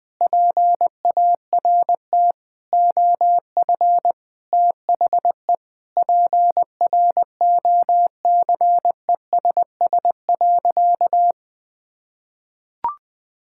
Morse code, heard as a keyed tone, about 21 wpm, 700 Hz. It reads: PART OF THE PROCESS.
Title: Sentences with 2-Word Phrases - 20wpm